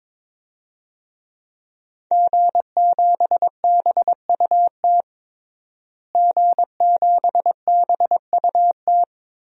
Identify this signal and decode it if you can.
Morse code, heard as a keyed tone, about 21 wpm, 700 Hz. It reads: G7BUT G7BUT
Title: DXCC-100 Call Signs - 5 Characters - Repeated 3x - 22wpm